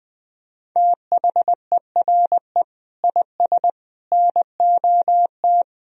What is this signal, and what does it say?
Morse code, heard as a keyed tone, about 20 wpm, 700 Hz. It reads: THERE IS NOT
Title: Sentences from Top 1000 Words - Review Part 4 - 20wpm